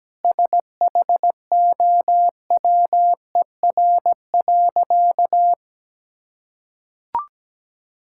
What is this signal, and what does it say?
Morse code, heard as a keyed tone, about 17 wpm, 700 Hz. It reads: SHOWER.
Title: Sentences from Top 1600 Words 17wpm